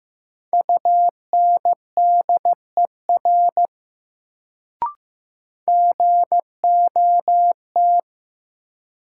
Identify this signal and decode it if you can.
Morse code, heard as a keyed tone, about 15 wpm, 700 Hz. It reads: UNDER GOT
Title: Sets of 2 Words - New Word Drill - 15wpm